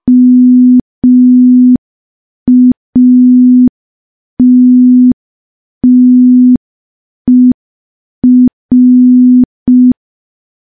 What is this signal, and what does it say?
Morse code, heard as a keyed tone, about 5 wpm, 250 Hz. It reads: MATTER
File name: morse / morse-training-05WPM